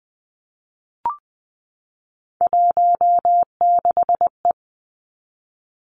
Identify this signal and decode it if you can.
Morse code, heard as a keyed tone, about 20 wpm, 700 Hz. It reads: 16E
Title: Three Challenging Letter-Number Combinations 20wpm